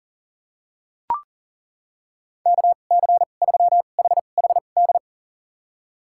Morse code, heard as a keyed tone, about 40 wpm, spaced as for 29 wpm, 700 Hz. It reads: KC3HHD
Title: USA Call Signs - 6 Characters - 2x Character Spacing - 40wpm